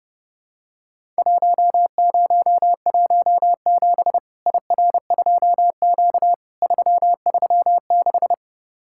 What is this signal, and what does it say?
Morse code, heard as a keyed tone, about 30 wpm, 700 Hz. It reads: 1017 IR2Q 336
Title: Worked All Europe DX Contest - QTC Exchange - 30wpm